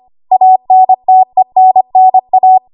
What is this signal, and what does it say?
Morse code, heard as a keyed tone, about 25 wpm, 750 Hz. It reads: ANTENNA
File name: Archive / 25 WPM / 25WPM